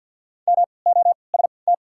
Morse code, heard as a keyed tone, about 52 wpm, spaced as for 29 wpm, 700 Hz. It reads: MOST
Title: 3x Character Spacing - Top 200 Words - 50wpm